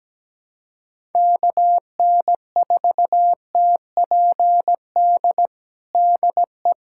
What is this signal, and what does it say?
Morse code, heard as a keyed tone, about 17 wpm, 700 Hz. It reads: KN4TPD DE
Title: USA Callsign DE Callsign 17wpm